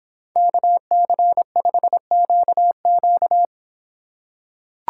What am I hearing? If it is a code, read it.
Morse code, heard as a keyed tone, about 24 wpm, 700 Hz. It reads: KC5QQ